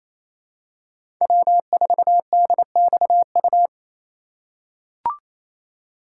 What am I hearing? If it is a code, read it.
Morse code, heard as a keyed tone, about 27 wpm, 700 Hz. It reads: W4DXU
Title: USA Call Signs - 5 Characters - 4x Character Spacing - 28wpm